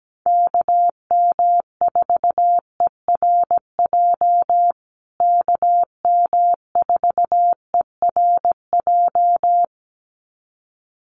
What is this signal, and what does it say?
Morse code, heard as a keyed tone, about 17 wpm, 700 Hz. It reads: KM4ERJ KM4ERJ
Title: USA Call Signs Repeated 2x 17wpm